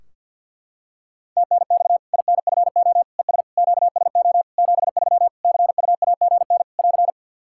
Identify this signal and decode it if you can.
Morse code, heard as a keyed tone, about 52 wpm, 700 Hz. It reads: TNX INFO ES QSO 73 CUAGN <AR>
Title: K1USN SST QSO Practice 50wpm